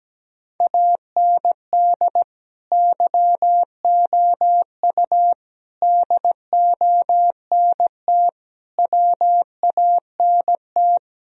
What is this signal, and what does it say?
Morse code, heard as a keyed tone, about 17 wpm, 700 Hz. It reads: AND YOU DONT WANT
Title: Sentences from Top 1000 Words - Review Part 2 - 17wpm